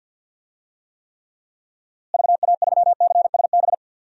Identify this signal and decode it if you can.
Morse code, heard as a keyed tone, about 46 wpm, 700 Hz. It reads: VA3CSB
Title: DXCC-100 Call Signs - 6 Characters - Repeated 3x - 50wpm